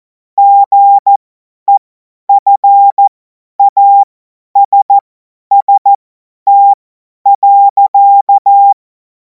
Morse code, heard as a keyed tone, about 14 wpm, spaced as for 10 wpm, 800 Hz. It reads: GEFASST.